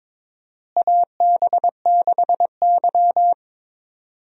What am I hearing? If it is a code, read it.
Morse code, heard as a keyed tone, about 21 wpm, 700 Hz. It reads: AB6Y
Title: USA Call Signs - 4 Characters - Repeated 2x - 22wpm